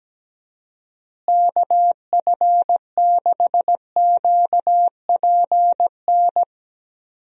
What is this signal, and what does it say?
Morse code, heard as a keyed tone, about 17 wpm, 700 Hz. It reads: KF6QPN
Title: USA Call Signs - 6 Characters - Repeated 2x - 17wpm